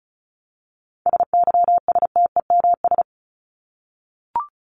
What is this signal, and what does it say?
Morse code, heard as a keyed tone, about 34 wpm, 700 Hz. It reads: SYSTEMS
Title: ES:S Suffix - Top 100 Words with Base Words 35wpm